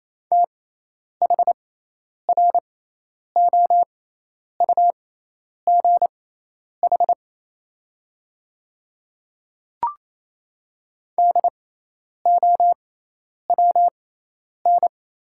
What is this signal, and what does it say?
Morse code, heard as a keyed tone, about 28 wpm, spaced as for 10 wpm, 700 Hz. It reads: THROUGH DOWN
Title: 6x Character Spacing - Top 500 Words - 28wpm